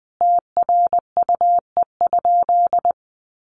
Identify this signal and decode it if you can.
Morse code, heard as a keyed tone, about 20 wpm, 700 Hz. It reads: TRUE?
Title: Sentences from Top 1400 Words 20wpm